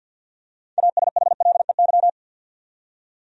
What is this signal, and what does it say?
Morse code, heard as a keyed tone, about 52 wpm, 700 Hz. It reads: ASHLEY
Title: USA Common Names - 50wpm